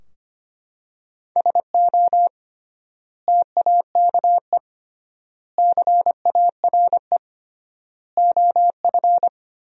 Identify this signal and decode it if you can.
Morse code, heard as a keyed tone, about 25 wpm, 700 Hz. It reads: SO TAKE CARE OF
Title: 3x Word-Spacing - Sentences from Top 300 Words - 25wpm